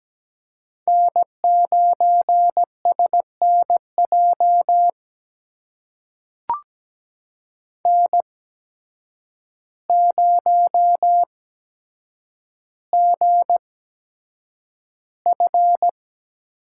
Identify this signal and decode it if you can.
Morse code, heard as a keyed tone, about 17 wpm, spaced as for 8 wpm, 700 Hz. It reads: N9SNJ N0GF
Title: USA Call Signs - 5 Characters - 8x Character Spacing - 17wpm